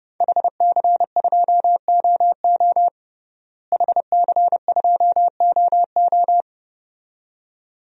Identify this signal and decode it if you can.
Morse code, heard as a keyed tone, about 32 wpm, 700 Hz. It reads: HC2OO HC2OO
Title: DXCC-100 Call Signs - 5 Characters - Repeated 2x - 30wpm